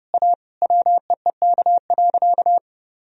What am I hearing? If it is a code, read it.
Morse code, heard as a keyed tone, about 30 wpm, 700 Hz. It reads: A WEEK.
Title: Sentences from Top 1000 Words - Review Part 2 - 30wpm